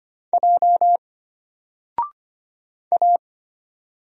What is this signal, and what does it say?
Morse code, heard as a keyed tone, about 25 wpm, 700 Hz. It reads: J A
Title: Single Letter-Number 25wpm